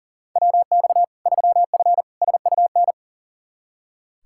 Morse code, heard as a keyed tone, about 37 wpm, 700 Hz. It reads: WX 3F SUN
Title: Practice QSO - Exchange 2 - Info - 40wpm